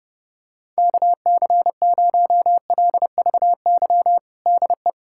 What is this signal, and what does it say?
Morse code, heard as a keyed tone, about 32 wpm, 700 Hz. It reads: KC0LVY DE
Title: USA Callsign DE Callsign 30wpm